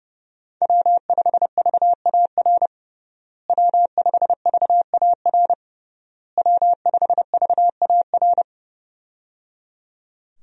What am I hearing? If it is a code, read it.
Morse code, heard as a keyed tone, about 30 wpm, 700 Hz. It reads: W5VAR W5VAR W5VAR
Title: USA Call Signs - 5 Characters - Repeated 3x - 30wpm